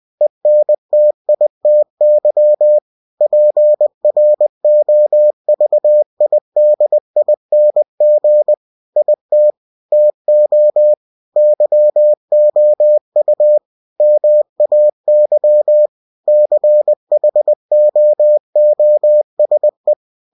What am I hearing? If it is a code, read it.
Morse code, heard as a keyed tone, about 20 wpm, 600 Hz. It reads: ENTITY PROVIDING IT TO YOU MAY CHOOSE
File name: cw-books / winnie-the-pooh / winnie-20wpm-page-0119